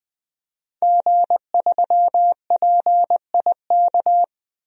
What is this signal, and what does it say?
Morse code, heard as a keyed tone, about 20 wpm, 700 Hz. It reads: G3PIK
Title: DXCC-100 Call Signs - 5 Characters - 20wpm